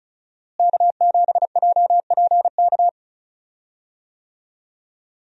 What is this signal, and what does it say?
Morse code, heard as a keyed tone, about 36 wpm, 700 Hz. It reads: K7JPK
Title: USA Call Signs - 5 Characters - Repeated 2x - 35wpm